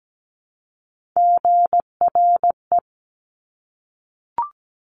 Morse code, heard as a keyed tone, about 17 wpm, 700 Hz. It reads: GRE
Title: Top Three Letter Combinations in Words 17wpm